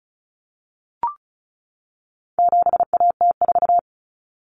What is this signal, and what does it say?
Morse code, heard as a keyed tone, about 34 wpm, 700 Hz. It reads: 7AT4
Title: Common Cut Numbers - 4-Digit 35wpm